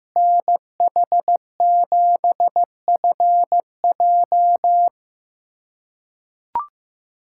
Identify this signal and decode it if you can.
Morse code, heard as a keyed tone, about 15 wpm, 700 Hz. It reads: NH7FJ
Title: USA Callsign DE Callsign 15wpm